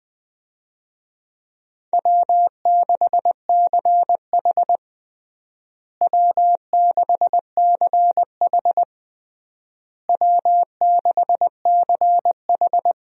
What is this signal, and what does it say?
Morse code, heard as a keyed tone, about 20 wpm, 700 Hz. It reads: W6CH W6CH W6CH
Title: USA Call Signs - 4 Characters - Repeated 3x - 20wpm